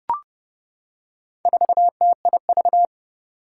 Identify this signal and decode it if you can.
Morse code, heard as a keyed tone, about 30 wpm, 700 Hz. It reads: 4TIV
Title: Four Challenging Letter-Number Combinations 30wpm